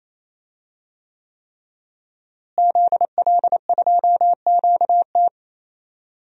Morse code, heard as a keyed tone, about 27 wpm, 700 Hz. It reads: ZL2QT
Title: DXCC-100 Call Signs - 5 Characters - Repeated 3x - 28wpm